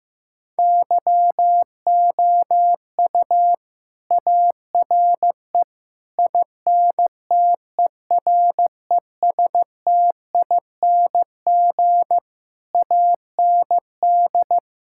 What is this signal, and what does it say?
Morse code, heard as a keyed tone, about 15 wpm, 700 Hz. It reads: YOU ARE INTERESTING AND
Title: Sentences from Top 1000 Words - Review Part 6 - 15wpm